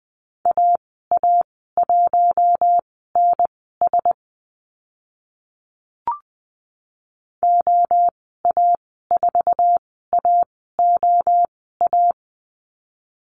Morse code, heard as a keyed tone, about 20 wpm, spaced as for 14 wpm, 700 Hz. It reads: AA1NS OA4AOA
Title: 048 Lesson 9g - Number 4 - Call Signs 20wpm